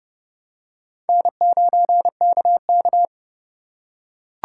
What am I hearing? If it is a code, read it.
Morse code, heard as a keyed tone, about 29 wpm, 700 Hz. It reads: N9KK